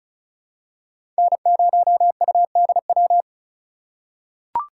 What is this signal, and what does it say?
Morse code, heard as a keyed tone, about 36 wpm, 700 Hz. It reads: N0UDW
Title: USA Call Signs - 5 Characters - 4x Character Spacing - 35wpm